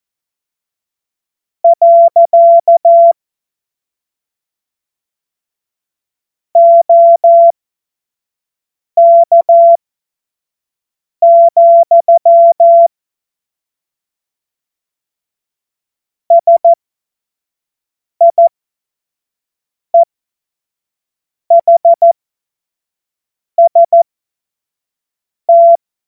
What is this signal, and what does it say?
Morse code, heard as a keyed tone, about 14 wpm, spaced as for 5 wpm, 680 Hz. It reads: . OK, SIEHST
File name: Morsekurs / Witze / Witz 06. April 2021, 25 F70